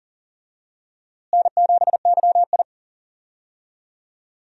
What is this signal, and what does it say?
Morse code, heard as a keyed tone, about 40 wpm, 700 Hz. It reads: N7YI
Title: USA Call Signs - 4 Characters - Repeated 2x - 40wpm